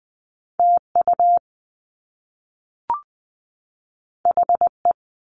Morse code, heard as a keyed tone, about 20 wpm, 700 Hz. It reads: TU HE
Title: Top Two Letter Combinations in Words 20wpm